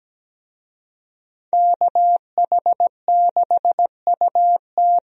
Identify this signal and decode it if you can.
Morse code, heard as a keyed tone, about 17 wpm, 700 Hz. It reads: KH6UT
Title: USA Call Signs - 5 Characters - Repeated 2x - 17wpm